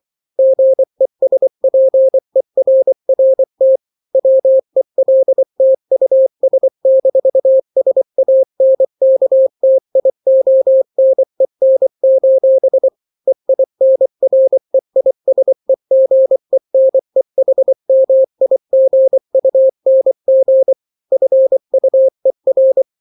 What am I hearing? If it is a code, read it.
Morse code, heard as a keyed tone, about 24 wpm, 530 Hz. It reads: GESPERRT WELT US-SANKTIONEN: EINREISEGENEHMIGUNG FUER
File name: daily / 23wpm